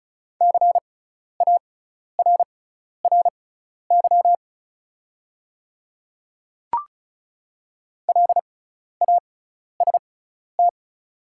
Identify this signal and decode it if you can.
Morse code, heard as a keyed tone, about 36 wpm, spaced as for 12 wpm, 700 Hz. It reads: CARRY LAST
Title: 6x Character Spacing - Top 300 Words - 35wpm